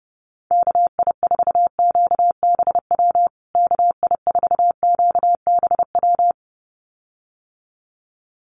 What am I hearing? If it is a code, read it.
Morse code, heard as a keyed tone, about 30 wpm, 700 Hz. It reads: KI4QBW KI4QBW
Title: USA Call Signs Repeated 2x 30wpm